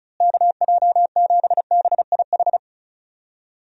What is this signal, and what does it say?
Morse code, heard as a keyed tone, about 34 wpm, 700 Hz. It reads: KJ7BIH
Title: USA Call Signs - 6 Characters - Repeated 3x - 35wpm